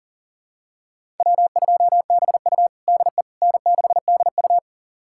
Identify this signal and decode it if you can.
Morse code, heard as a keyed tone, about 40 wpm, 700 Hz. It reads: W2BU DE N6DU
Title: USA Callsign DE Callsign - 4 Characters - 40wpm